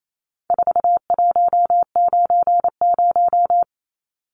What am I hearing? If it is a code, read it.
Morse code, heard as a keyed tone, about 28 wpm, 700 Hz. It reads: 4190
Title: Numbers - 4-Digit 28wpm